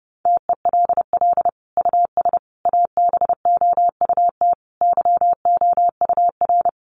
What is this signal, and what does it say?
Morse code, heard as a keyed tone, about 28 wpm, 700 Hz. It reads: TELL US ABOUT YOUR
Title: Sentences from Top 1200 Words 30wpm